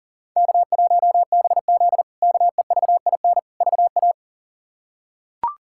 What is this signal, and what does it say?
Morse code, heard as a keyed tone, about 40 wpm, 700 Hz. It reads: K1BZ KEVIN VA
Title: K1USN SST QSO Practice - 40wpm